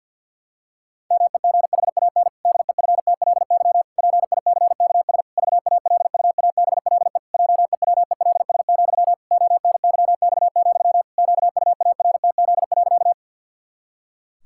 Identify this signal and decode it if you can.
Morse code, heard as a keyed tone, about 53 wpm, 700 Hz. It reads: MEGHAN DEFTLY PICKS VALUABLE JEWELS, ONYX, QUARTZ.
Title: Pangram Challenge v2 - 50wpm